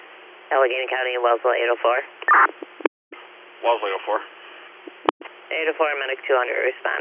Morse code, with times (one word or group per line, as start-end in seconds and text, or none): none